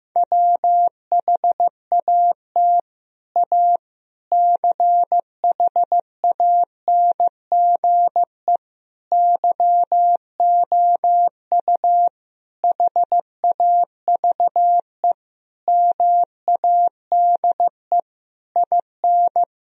Morse code, heard as a keyed tone, about 15 wpm, 700 Hz. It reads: WHAT A CHANGE YOU HAVE MADE IN
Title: Sentences from Top 1000 Words - Review Part 1 - 15wpm